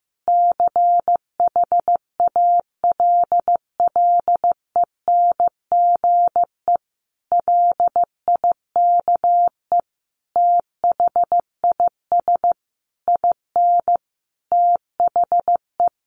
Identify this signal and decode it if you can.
Morse code, heard as a keyed tone, about 15 wpm, 700 Hz. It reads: CHALLENGE LIKE THIS IN THE
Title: Sentences from Top 1500 Words 15wpm